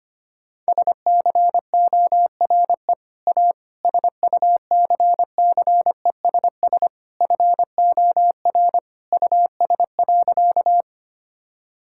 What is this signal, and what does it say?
Morse code, heard as a keyed tone, about 25 wpm, 700 Hz. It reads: SCORE A SUCCESS FOR US.
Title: Sentences from Top 1600 Words 25wpm